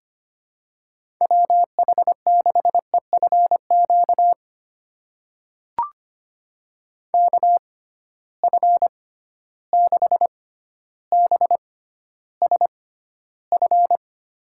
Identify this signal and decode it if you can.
Morse code, heard as a keyed tone, about 25 wpm, spaced as for 13 wpm, 700 Hz. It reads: WH6EFQ KF6BSF
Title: USA Call Signs - 6 Characters - 6x Character Spacing - 25wpm